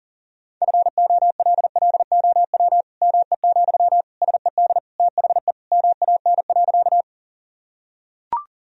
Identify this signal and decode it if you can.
Morse code, heard as a keyed tone, about 40 wpm, 700 Hz. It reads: FOLLOW ME, SED THE MAN.
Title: Sentences from Top 200 Words and Abbreviations 40wpm